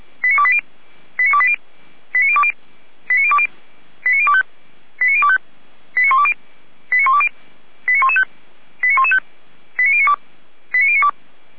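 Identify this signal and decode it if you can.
Five-tone selective call (ZVEI), heard as quick runs of five tones, tones of about 70 ms. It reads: 89280 89280 89020 89020 89025 89025 89120 89120 89115 89115 89092 89092
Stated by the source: ZVEI five-tone selective call 89262, received by SDR, for the fire brigade (Feuerwehr) Orlamünde.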